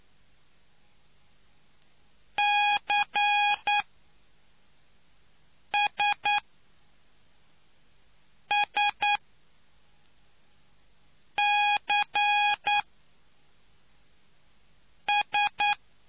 Morse code, heard as a keyed tone, about 9 wpm, spaced as for 3 wpm, 830 Hz. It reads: CSSCS